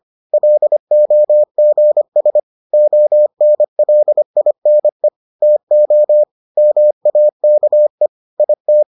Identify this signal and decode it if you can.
Morse code, heard as a keyed tone, about 25 wpm, 600 Hz. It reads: LOGS ONLINE TO MAKE IT